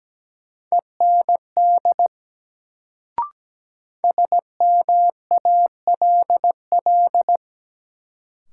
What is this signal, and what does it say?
Morse code, heard as a keyed tone, about 17 wpm, 700 Hz. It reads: END SMALL
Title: Sentences from Top 200 Words - New Word Drill - 17wpm